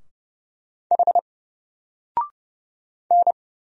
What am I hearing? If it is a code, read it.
Morse code, heard as a keyed tone, about 30 wpm, 700 Hz. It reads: H N
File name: Single-Letter-Number-30wpm